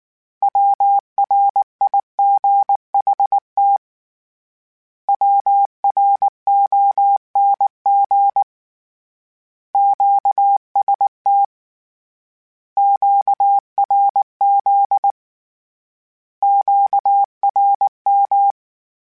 Morse code, heard as a keyed tone, about 19 wpm, 800 Hz. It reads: WRIGHT WRONG QST QRZ QRM
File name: Misc / Misc-mp3s / Words @18